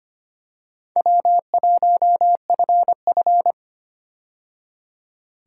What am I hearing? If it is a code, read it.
Morse code, heard as a keyed tone, about 25 wpm, 700 Hz. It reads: W1FF
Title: USA Call Signs - 4 Characters - 25wpm